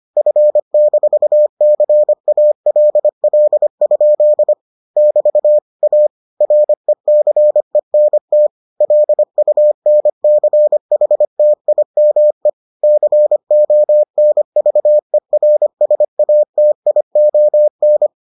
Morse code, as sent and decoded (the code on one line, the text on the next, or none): ..-. -....- -.-. .- .-.. .-.. ..--.. / -...- / .- / .-. -.-. -. - / .-.. ..- -. -.-. .... - .. -- . / -.-. --- -. ...- . .-. ... .- - .. --- -.
F-CALL? <BT> A RECENT LUNCHTIME CONVERSATION